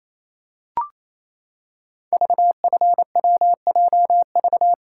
Morse code, heard as a keyed tone, about 27 wpm, 700 Hz. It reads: VFWJV
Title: Code Groups - 5 Letters - 28wpm